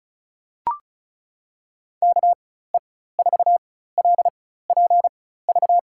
Morse code, heard as a keyed tone, about 36 wpm, spaced as for 16 wpm, 700 Hz. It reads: KE4LPV
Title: USA Call Signs - 4x Character Spacing - 35wpm